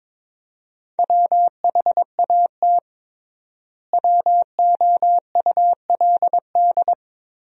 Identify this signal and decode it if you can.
Morse code, heard as a keyed tone, about 22 wpm, 700 Hz. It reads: WHAT WOULD
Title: 3x Word-Spacing - Sentences from Top 300 Words - 22wpm